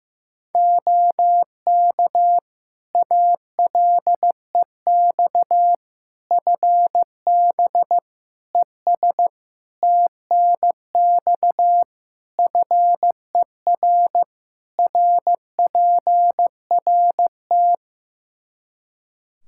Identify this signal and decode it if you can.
Morse code, heard as a keyed tone, about 15 wpm, 700 Hz. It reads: OK ALEX FB ES TNX FER RPRT
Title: Practice QSO - Exchange 2 - Info - 15wpm